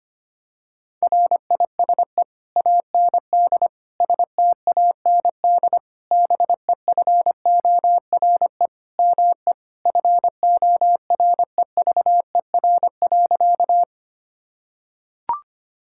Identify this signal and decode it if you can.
Morse code, heard as a keyed tone, about 25 wpm, 700 Hz. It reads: RISE AND STAND BEFORE ME FOREVER.
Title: Sentences from Top 1600 Words 25wpm